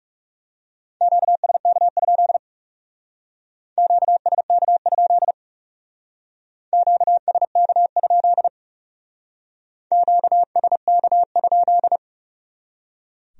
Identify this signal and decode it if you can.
Morse code, heard as a keyed tone, about 36 wpm, 700 Hz. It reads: QSK? QSK? QSK? QSK?
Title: Speed-Racing Q-Codes - 30wpm